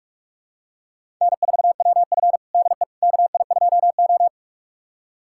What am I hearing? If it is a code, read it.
Morse code, heard as a keyed tone, about 47 wpm, 700 Hz. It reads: N4WF DE KI2O
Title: USA Callsign DE Callsign - 4 Characters - 45wpm